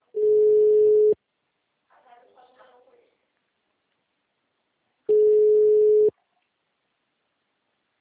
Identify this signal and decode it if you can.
Ringback tone (Europe, 425 Hz), with speech.